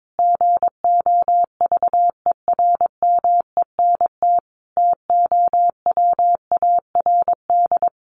Morse code, heard as a keyed tone, about 22 wpm, 700 Hz. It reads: GOVERMENT TOWARD
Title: Sentences from Top 1300 Words 22wpm